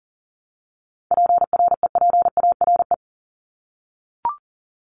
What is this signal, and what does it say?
Morse code, heard as a keyed tone, about 40 wpm, 700 Hz. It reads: PREPARE
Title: Top 751-1000 Words 40wpm